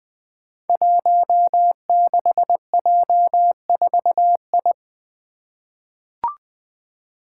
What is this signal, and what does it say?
Morse code, heard as a keyed tone, about 20 wpm, 700 Hz. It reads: 16J4I
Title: Five Challenging Letter-Number Combinations 20wpm